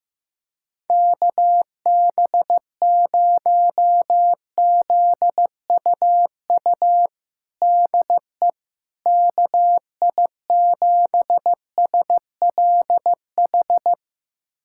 Morse code, heard as a keyed tone, about 15 wpm, 700 Hz. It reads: KB0ZUU DE KI7SLH
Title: USA Callsign DE Callsign - 6 Characters - 15wpm